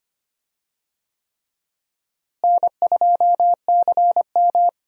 Morse code, heard as a keyed tone, about 25 wpm, 700 Hz. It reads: N2CM
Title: USA Call Signs - 4 Characters - Repeated 3x - 25wpm